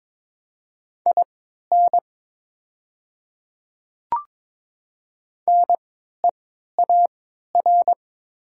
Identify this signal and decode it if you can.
Morse code, heard as a keyed tone, about 22 wpm, spaced as for 12 wpm, 700 Hz. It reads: IN NEAR
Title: 3x Character Spacing - Top 300 Words - 22wpm